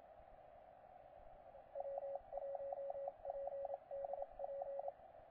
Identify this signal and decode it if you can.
Morse code, heard as a keyed tone, about 27 wpm, 560 Hz, with radio noise. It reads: W1PDP
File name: posts / 9l1jt-15cw